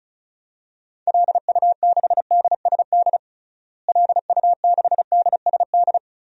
Morse code, heard as a keyed tone, about 34 wpm, 700 Hz. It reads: LU6DSD LU6DSD
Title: DXCC-100 Call Signs - 6 Characters - Repeated 3x - 35wpm